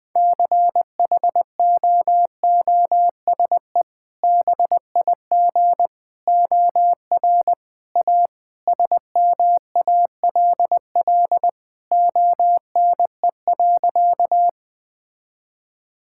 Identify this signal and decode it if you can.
Morse code, heard as a keyed tone, about 20 wpm, 700 Hz. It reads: CHOOSE BIG OR A SMALL ONE.
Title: Sentences with 2-Word Phrases - 20wpm